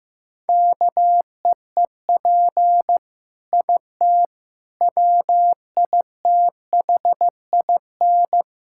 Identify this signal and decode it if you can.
Morse code, heard as a keyed tone, about 15 wpm, 700 Hz. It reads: KEEP IT WITHIN
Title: Sentences from Top 1000 Words - Review Part 6 - 15wpm